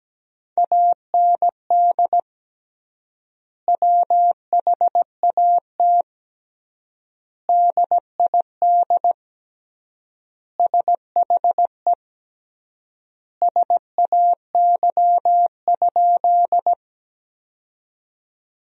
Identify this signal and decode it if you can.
Morse code, heard as a keyed tone, about 17 wpm, 700 Hz. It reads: AND WHAT DID SHE SAY?
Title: 3x Word-Spacing - Sentences from Top 200 Words - 17wpm